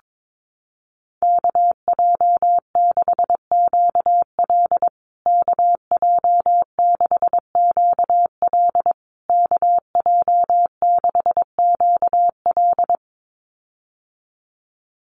Morse code, heard as a keyed tone, about 22 wpm, 700 Hz. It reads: KJ6QL KJ6QL KJ6QL
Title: USA Call Signs Repeated 3x 22wpm